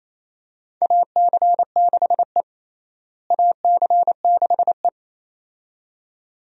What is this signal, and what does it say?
Morse code, heard as a keyed tone, about 28 wpm, 700 Hz. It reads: AC6E AC6E
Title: USA Call Signs - 4 Characters - Repeated 2x - 28wpm